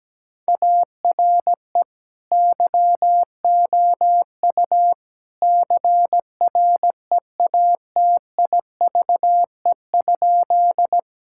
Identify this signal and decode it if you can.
Morse code, heard as a keyed tone, about 17 wpm, 700 Hz. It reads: ARE YOU CREATIVE?
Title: Sentences from Top 1000 Words - Review Part 5 - 17wpm